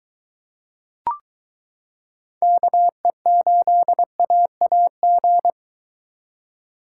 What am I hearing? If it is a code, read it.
Morse code, heard as a keyed tone, about 22 wpm, 700 Hz. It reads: KE8AAG